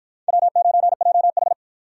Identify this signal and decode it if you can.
Morse code, heard as a keyed tone, about 53 wpm, 700 Hz. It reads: W9JH